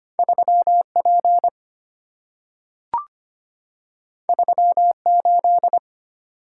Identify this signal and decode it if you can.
Morse code, heard as a keyed tone, about 25 wpm, 700 Hz. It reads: 3P 38
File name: Two Letter-Number Combinations - 25wpm